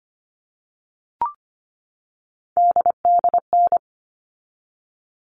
Morse code, heard as a keyed tone, about 25 wpm, 700 Hz. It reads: DDN
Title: All Cut Numbers - 3-Digit 25wpm